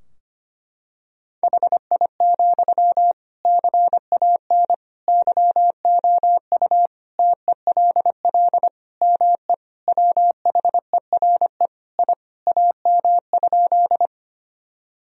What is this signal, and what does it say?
Morse code, heard as a keyed tone, about 25 wpm, 700 Hz. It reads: HI, CAN YOU TELL ME WHERE I AM?
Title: Sentences from Top 1000 Words - Review Part 3 - 25wpm